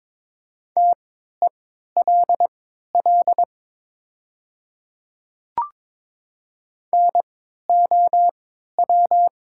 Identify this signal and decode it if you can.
Morse code, heard as a keyed tone, about 22 wpm, spaced as for 13 wpm, 700 Hz. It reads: TELL NOW
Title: 3x Character Spacing - Top 200 Words - 22wpm